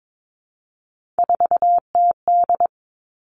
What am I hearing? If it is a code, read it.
Morse code, heard as a keyed tone, about 22 wpm, 700 Hz. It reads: 4TD
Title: Three Challenging Letter-Number Combinations 22wpm